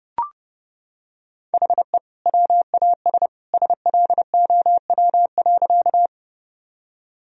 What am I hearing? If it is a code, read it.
Morse code, heard as a keyed tone, about 32 wpm, 700 Hz. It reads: HE WAS SLOW.
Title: Sentences from Top 1000 Words - Review Part 2 - 30wpm